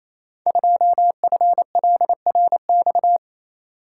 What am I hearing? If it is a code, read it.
Morse code, heard as a keyed tone, about 27 wpm, 700 Hz. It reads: 2FLRX
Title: Code Groups - 5 Letter:Numbers - 28wpm